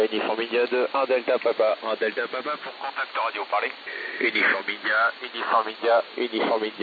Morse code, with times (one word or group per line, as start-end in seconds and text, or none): none